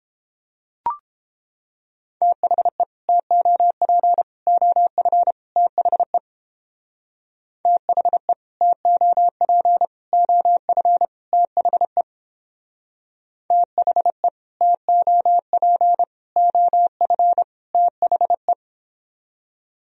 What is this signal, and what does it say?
Morse code, heard as a keyed tone, about 29 wpm, 700 Hz. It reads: THE TOP OF THE THE TOP OF THE THE TOP OF THE